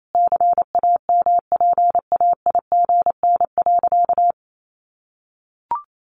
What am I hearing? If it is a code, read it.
Morse code, heard as a keyed tone, about 28 wpm, 700 Hz. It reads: CAMPAIGN.
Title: Sentences from Top 1200 Words 28wpm